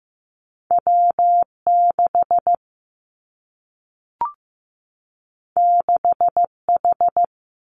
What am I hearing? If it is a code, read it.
Morse code, heard as a keyed tone, about 15 wpm, 700 Hz. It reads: W 6 6 H
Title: Two Challenging Letter-Number Combinations 15wpm